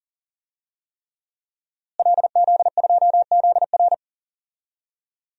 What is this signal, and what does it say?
Morse code, heard as a keyed tone, about 40 wpm, 700 Hz. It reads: LZ2ZR
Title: DXCC-100 Call Signs - 5 Characters - Repeated 3x - 40wpm